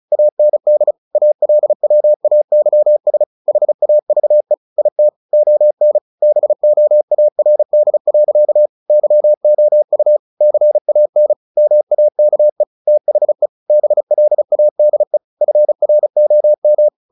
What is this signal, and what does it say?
Morse code, heard as a keyed tone, about 34 wpm, 600 Hz. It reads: AND ALWAYS HAVE IT ON BOARD. YOU CAN MAKE THE BLADE FROM
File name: the old man and the sea 2 35wpm